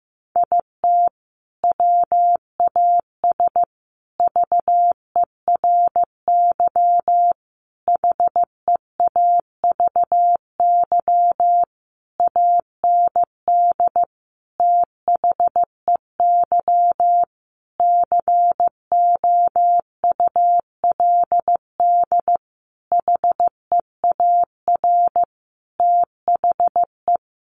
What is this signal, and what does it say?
Morse code, heard as a keyed tone, about 15 wpm, 700 Hz. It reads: IT WAS VERY HEAVY AND THEY COULD HEAR THE
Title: Sentences from Top 1100 Words 15wpm